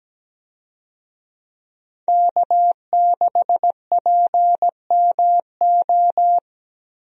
Morse code, heard as a keyed tone, about 17 wpm, 700 Hz. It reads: K6PMO
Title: USA Call Signs - 5 Characters - Repeated 2x - 17wpm